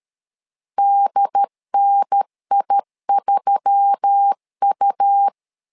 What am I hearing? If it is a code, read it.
Morse code, heard as a keyed tone, about 13 wpm, 800 Hz. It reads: DNI3U
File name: LEZIONI / 5 LBVC / 5 LB 12wpm 1906